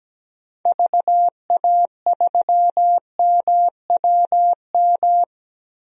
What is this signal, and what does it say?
Morse code, heard as a keyed tone, about 17 wpm, 700 Hz. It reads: VA3MWM
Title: DXCC-100 Call Signs - 6 Characters - 17wpm